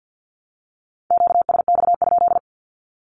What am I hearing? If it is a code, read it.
Morse code, heard as a keyed tone, about 50 wpm, 700 Hz. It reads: QSX?
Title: Q-Codes 50wpm